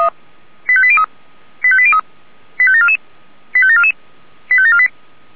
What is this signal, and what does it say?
Five-tone selective call (ZVEI), heard as quick runs of five tones, tones of about 70 ms. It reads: 85803 85803 85744 85744 85748